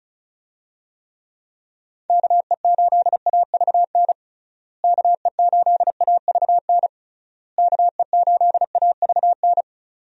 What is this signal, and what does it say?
Morse code, heard as a keyed tone, about 34 wpm, 700 Hz. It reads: KE8AVN KE8AVN KE8AVN
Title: USA Call Signs - 6 Characters - Repeated 3x - 35wpm